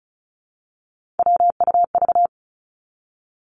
Morse code, heard as a keyed tone, about 36 wpm, 700 Hz. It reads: WUV
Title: Three Challenging Letter-Number Combinations 35wpm